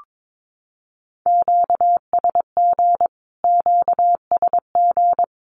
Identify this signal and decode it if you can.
Morse code, heard as a keyed tone, about 22 wpm, 700 Hz. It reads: QSG QSG
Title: Q-Codes Repeated 3x 22wpm